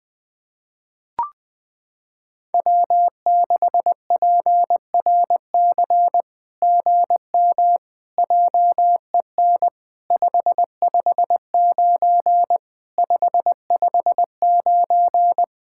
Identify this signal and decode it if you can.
Morse code, heard as a keyed tone, about 20 wpm, 700 Hz. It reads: W6PRC GM JEN 559 559
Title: SOTA QSO Practice 20wpm